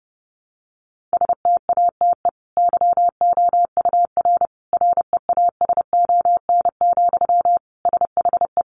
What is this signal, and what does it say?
Morse code, heard as a keyed tone, about 30 wpm, 700 Hz. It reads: STATE YOUR REASON, SHE
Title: Sentences from Top 700 Words 30wpm